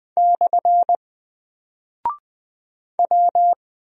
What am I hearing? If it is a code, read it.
Morse code, heard as a keyed tone, about 20 wpm, 700 Hz. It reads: / W
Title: Single Character (Letter:Number:Punctuation:Prosign) 20wpm